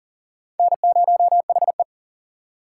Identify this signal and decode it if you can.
Morse code, heard as a keyed tone, about 40 wpm, 700 Hz. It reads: N0HE
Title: USA Call Signs - 4 Characters - 3x Character Spacing - 40wpm